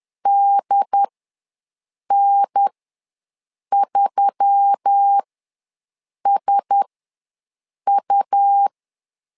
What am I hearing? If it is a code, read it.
Morse code, heard as a keyed tone, about 10 wpm, spaced as for 6 wpm, 800 Hz. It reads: DN3SU
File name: LEZIONI / 4 Q1S3 ws 10wpm 2437